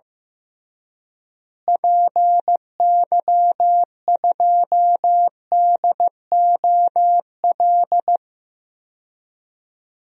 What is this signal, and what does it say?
Morse code, heard as a keyed tone, about 15 wpm, 700 Hz. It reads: PY2DOL
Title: DXCC-100 Call Signs - 6 Characters - Repeated 2x - 15wpm